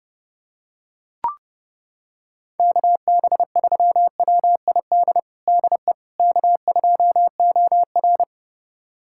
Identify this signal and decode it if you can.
Morse code, heard as a keyed tone, about 32 wpm, 700 Hz. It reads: KB3WID DE K2OR
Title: USA Callsign DE Callsign 30wpm